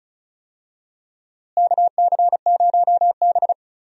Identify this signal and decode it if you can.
Morse code, heard as a keyed tone, about 36 wpm, 700 Hz. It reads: KC0B
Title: USA Call Signs - 4 Characters - Repeated 2x - 35wpm